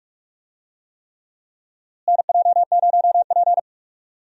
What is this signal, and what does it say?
Morse code, heard as a keyed tone, about 43 wpm, 700 Hz. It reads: NJ0P
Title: USA Call Signs - 4 Characters - Repeated 3x - 45wpm